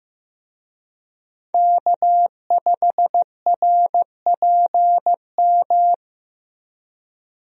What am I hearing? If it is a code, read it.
Morse code, heard as a keyed tone, about 15 wpm, 700 Hz. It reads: K5RPM
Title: USA Call Signs - 5 Characters - 15wpm